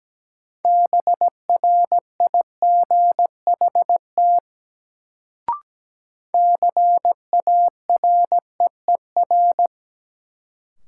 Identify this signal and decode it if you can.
Morse code, heard as a keyed tone, about 17 wpm, 700 Hz. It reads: BRIGHT CAREER
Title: Sentences from Top 1000 Words - New Word Drill - 17wpm